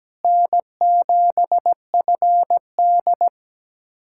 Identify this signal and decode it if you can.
Morse code, heard as a keyed tone, about 17 wpm, 700 Hz. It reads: N7FD
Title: USA Callsign DE Callsign - 4 Characters - 17wpm